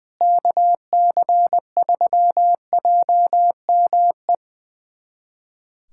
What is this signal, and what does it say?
Morse code, heard as a keyed tone, about 20 wpm, 700 Hz. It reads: KC3JME
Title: USA Callsign DE Callsign 20wpm